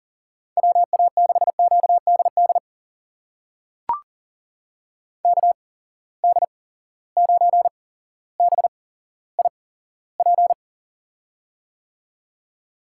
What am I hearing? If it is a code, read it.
Morse code, heard as a keyed tone, about 40 wpm, spaced as for 17 wpm, 700 Hz. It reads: WA6QDD KD9BIP
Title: USA Call Signs - 6 Characters - 8x Character Spacing - 40wpm